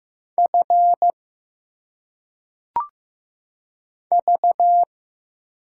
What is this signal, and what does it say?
Morse code, heard as a keyed tone, about 15 wpm, 700 Hz. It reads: F V
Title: Single Letters - Phonetic Pronunciation 15wpm